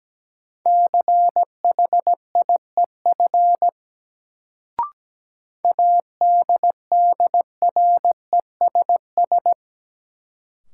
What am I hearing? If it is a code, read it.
Morse code, heard as a keyed tone, about 17 wpm, 700 Hz. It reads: CHIEF ADDRESS
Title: Sets of 5 Words Encore - New Word Drill - 17wpm